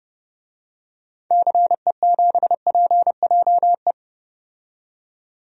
Morse code, heard as a keyed tone, about 30 wpm, 700 Hz. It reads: CE7PJE
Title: DXCC-100 Call Signs - 6 Characters - 30wpm